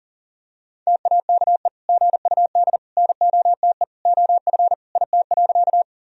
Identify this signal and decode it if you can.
Morse code, heard as a keyed tone, about 40 wpm, 700 Hz. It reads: TAKE GUD NOTE OF IT.
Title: Sentences from Top 500 Words and Abbreviations 40wpm